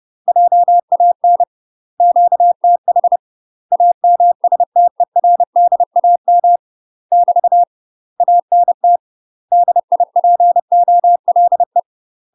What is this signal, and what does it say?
Morse code, heard as a keyed tone, about 30 wpm, 700 Hz. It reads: JAN QTH AMSTERDAM <BT> ANT DIPOLE
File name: UK MASTER QSOs TRAINING / 30WPM / UK MASTER QSOs 0020